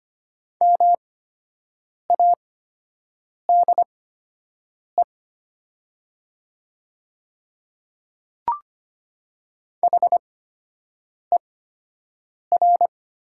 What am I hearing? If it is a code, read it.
Morse code, heard as a keyed tone, about 25 wpm, spaced as for 7 wpm, 700 Hz. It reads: MADE HER
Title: 8x Character Spacing - Top 200 Words - 25wpm